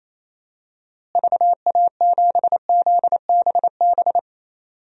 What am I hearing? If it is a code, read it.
Morse code, heard as a keyed tone, about 28 wpm, 700 Hz. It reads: VA7ZBB
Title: DXCC-100 Call Signs - 6 Characters - Repeated 2x - 28wpm